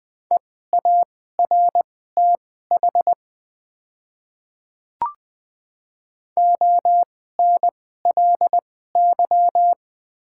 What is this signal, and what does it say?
Morse code, heard as a keyed tone, about 20 wpm, spaced as for 14 wpm, 700 Hz. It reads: EARTH ONLY
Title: 2x Character Spacing - Top 300 Words - 20wpm